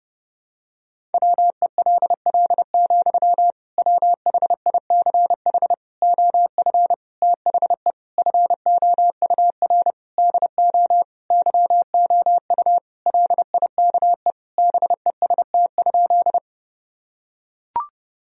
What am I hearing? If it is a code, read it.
Morse code, heard as a keyed tone, about 30 wpm, 700 Hz. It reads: WELL, WHICH OF THE FOUR DO YOU LIKE BEST?
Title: Sentences from Top 1000 Words - Review Part 1 - 30wpm